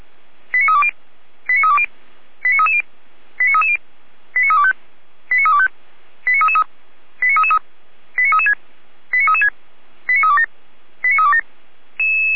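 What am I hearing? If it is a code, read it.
Five-tone selective call (ZVEI), heard as quick runs of five tones, tones of about 70 ms. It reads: 89320 89320 89330 89330 89325 89325 89333 89336 89336 89327 89327
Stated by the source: ZVEI five-tone selective call received on SDR.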